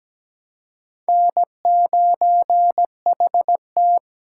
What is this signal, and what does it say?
Morse code, heard as a keyed tone, about 20 wpm, 700 Hz. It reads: N9HT
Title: Speed Racing - USA 1x2 Call Signs 17wpm